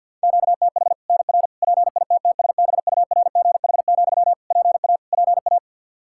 Morse code, heard as a keyed tone, about 52 wpm, 700 Hz. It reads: QTH NR PITTSBURGH, PA PA
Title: Practice QSO - Exchange 1 - Beginning - 50wpm